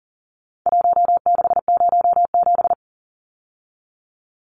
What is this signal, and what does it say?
Morse code, heard as a keyed tone, about 40 wpm, 700 Hz. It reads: 1607
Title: Numbers - Contest Style - 4-Digit 40wpm